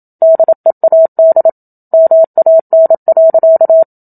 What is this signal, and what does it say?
Morse code, heard as a keyed tone, about 27 wpm, 650 Hz. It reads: DEAD MAN.
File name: Book Two / 01-Chapter 1 - 27 WPM